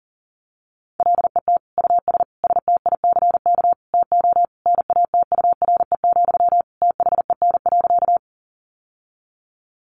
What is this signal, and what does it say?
Morse code, heard as a keyed tone, about 40 wpm, 700 Hz. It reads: LET US STICK TO NATURE, THEN.
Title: Sentences from Top 1200 Words 40wpm